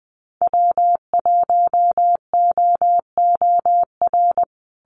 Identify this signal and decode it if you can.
Morse code, heard as a keyed tone, about 20 wpm, 700 Hz. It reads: W1OOR
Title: USA Call Signs Repeated 2x 20wpm